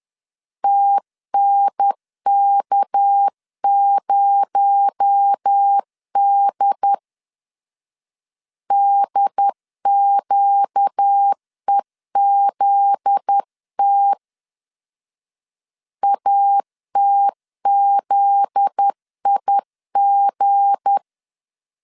Morse code, heard as a keyed tone, about 10 wpm, 800 Hz. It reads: TNK0D DQEZT ATZIG
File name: LEZIONI / 7 GF9Z / 7 GF9Z 10wpm 2611